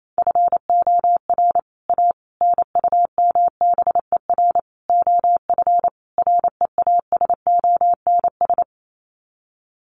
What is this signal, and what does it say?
Morse code, heard as a keyed tone, about 28 wpm, 700 Hz. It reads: FOR A NUMBER OF REASONS
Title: Sets of 5 Words Encore2 28wpm